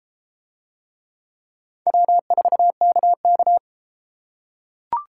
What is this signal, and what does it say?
Morse code, heard as a keyed tone, about 28 wpm, 700 Hz. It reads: W4KK